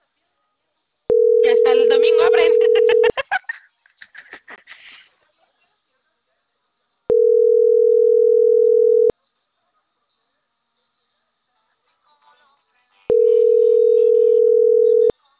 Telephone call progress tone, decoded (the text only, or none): ringback tone